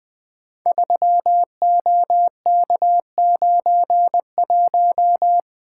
Morse code, heard as a keyed tone, about 20 wpm, 700 Hz. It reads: 3OK91
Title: Code Groups - 5 Letter:Numbers - 20wpm